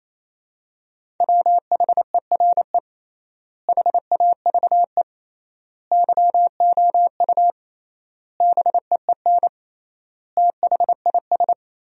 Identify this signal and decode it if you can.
Morse code, heard as a keyed tone, about 28 wpm, 700 Hz. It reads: WHERE HAVE YOU BEEN THIS
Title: 3x Word-Spacing - Sentences from Top 300 Words - 28wpm